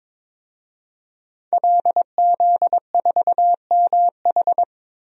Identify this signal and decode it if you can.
Morse code, heard as a keyed tone, about 22 wpm, 700 Hz. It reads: LZ4MH
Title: DXCC-100 Call Signs - 5 Characters - Repeated 3x - 22wpm